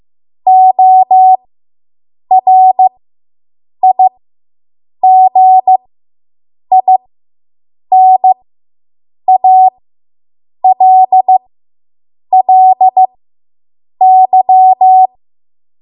Morse code, heard as a keyed tone, about 15 wpm, spaced as for 7 wpm, 750 Hz. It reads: ORIGINALLY